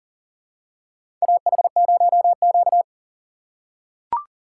Morse code, heard as a keyed tone, about 40 wpm, 700 Hz. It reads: AH0Q
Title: USA Call Signs - 4 Characters - 6x Character Spacing - 40wpm